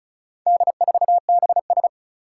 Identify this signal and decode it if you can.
Morse code, heard as a keyed tone, about 34 wpm, 700 Hz. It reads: D4BS
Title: Four Challenging Letter-Number Combinations 35wpm